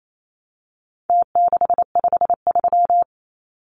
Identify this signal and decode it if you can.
Morse code, heard as a keyed tone, about 28 wpm, 700 Hz. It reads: T653
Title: Numbers - Contest Style - 4-Digit 28wpm